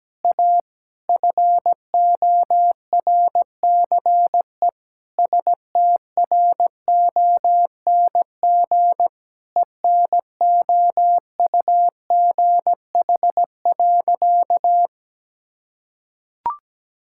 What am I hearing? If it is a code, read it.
Morse code, heard as a keyed tone, about 17 wpm, 700 Hz. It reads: A FORCE STRONG ENOUGH.
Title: Sentences from Top 1600 Words 17wpm